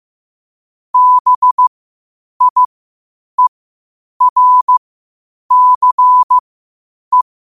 Morse code, heard as a keyed tone, about 15 wpm, spaced as for 8 wpm, 1000 Hz. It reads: BIERCE
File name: qotdfiles / 8wpm